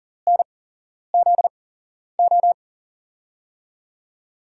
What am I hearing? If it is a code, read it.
Morse code, heard as a keyed tone, about 40 wpm, spaced as for 11 wpm, 700 Hz. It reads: NZO
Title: USA Call Signs - 6 Characters - 8x Character Spacing - 40wpm